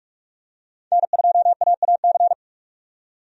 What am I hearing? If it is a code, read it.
Morse code, heard as a keyed tone, about 46 wpm, 700 Hz. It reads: N2AAC
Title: USA Call Signs - 5 Characters - 45wpm